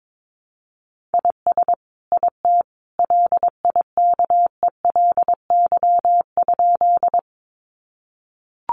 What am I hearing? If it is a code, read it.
Morse code, heard as a keyed tone, about 22 wpm, 700 Hz. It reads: IS IT LIKELY?
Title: Sentences from Top 600 Words 22wpm